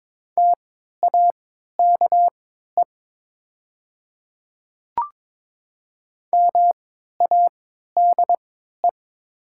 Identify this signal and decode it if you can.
Morse code, heard as a keyed tone, about 22 wpm, spaced as for 12 wpm, 700 Hz. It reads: TAKE MADE
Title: 3x Character Spacing - Top 200 Words - 22wpm